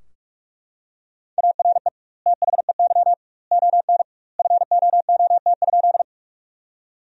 Morse code, heard as a keyed tone, about 44 wpm, 700 Hz. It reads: ARE THEY ON FOOT?
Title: 2x Word-Spacing - Sentences from Top 500 Words - 45wpm